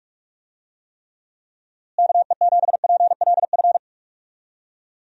Morse code, heard as a keyed tone, about 46 wpm, 700 Hz. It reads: KE7PLF
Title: USA Call Signs - 6 Characters - Repeated 3x - 45wpm